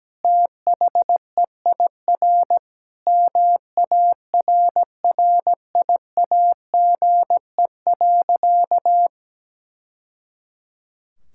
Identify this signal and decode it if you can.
Morse code, heard as a keyed tone, about 17 wpm, 700 Hz. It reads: THEIR MARRIAGE.
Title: Sentences from Top 1000 Words - Review Part 6 - 17wpm